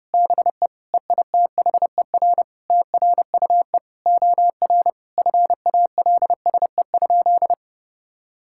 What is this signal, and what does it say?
Morse code, heard as a keyed tone, about 30 wpm, 700 Hz. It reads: BE EITHER TRUE OR FALSE?
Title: Sentences from Top 1600 Words 30wpm